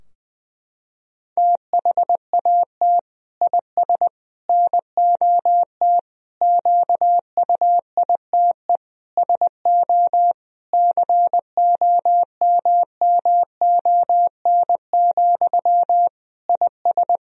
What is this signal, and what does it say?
Morse code, heard as a keyed tone, about 20 wpm, 700 Hz. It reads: THAT IS NOT QUITE SO COMMON, IS
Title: Sentences from Top 1000 Words - Review Part 3 - 20wpm